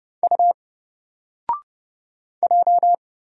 Morse code, heard as a keyed tone, about 30 wpm, 700 Hz. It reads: U J
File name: Single-Letter-Number-30wpm